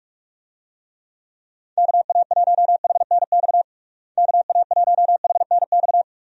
Morse code, heard as a keyed tone, about 43 wpm, 700 Hz. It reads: KA1HNX KA1HNX
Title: USA Call Signs - 6 Characters - Repeated 2x - 45wpm